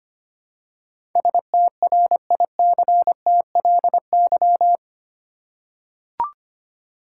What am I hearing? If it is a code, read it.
Morse code, heard as a keyed tone, about 25 wpm, 700 Hz. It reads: STRICTLY
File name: 8 Letter Words - First 3 Letters Given - 25wpm